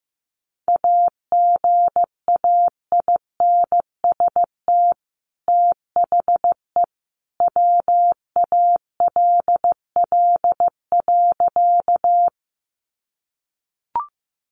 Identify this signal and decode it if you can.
Morse code, heard as a keyed tone, about 15 wpm, 700 Hz. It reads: AGAINST THE WALL.
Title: Sentences from Top 1100 Words 15wpm